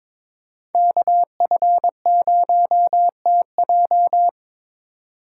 Morse code, heard as a keyed tone, about 22 wpm, 700 Hz. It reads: KF0TJ
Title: USA Call Signs - 5 Characters - 3x Character Spacing - 22wpm